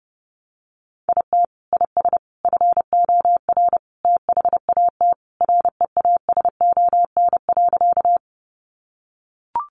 Morse code, heard as a keyed tone, about 30 wpm, 700 Hz. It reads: IT IS FOR THAT REASON.
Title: Sentences from Top 700 Words 30wpm